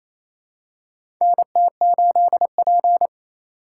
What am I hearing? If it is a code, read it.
Morse code, heard as a keyed tone, about 28 wpm, 700 Hz. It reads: NT8P
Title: USA Call Signs - 4x Character Spacing - 28wpm